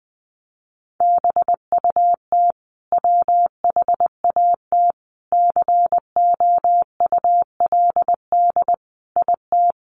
Morse code, heard as a keyed tone, about 20 wpm, 700 Hz. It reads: BUT WHAT COULD IT
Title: Sentences from Top 1300 Words 20wpm